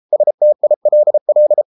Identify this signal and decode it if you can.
Morse code, heard as a keyed tone, about 33 wpm, 600 Hz. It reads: STILL